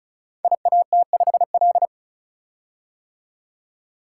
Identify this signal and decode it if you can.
Morse code, heard as a keyed tone, about 36 wpm, 700 Hz. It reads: IAT5L
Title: Code Groups - 5 Letter:Numbers - 35wpm